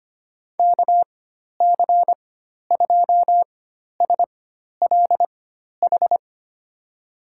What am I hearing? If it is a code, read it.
Morse code, heard as a keyed tone, about 25 wpm, spaced as for 12 wpm, 700 Hz. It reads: KC2SLH
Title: USA Call Signs - 6 Characters - 4x Character Spacing - 25wpm